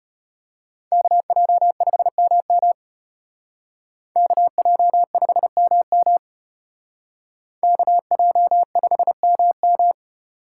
Morse code, heard as a keyed tone, about 34 wpm, 700 Hz. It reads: KJ5MM KJ5MM KJ5MM